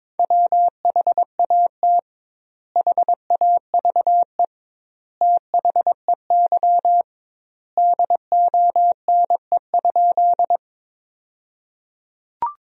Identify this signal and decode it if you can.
Morse code, heard as a keyed tone, about 22 wpm, 700 Hz. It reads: WHAT HAVE THEY DONE?
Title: 2x Word-Spacing - Sentences from Top 500 Words - 22wpm